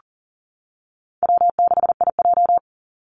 Morse code, heard as a keyed tone, about 40 wpm, 700 Hz. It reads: W6IJ
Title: Easy Call Signs - 40wpm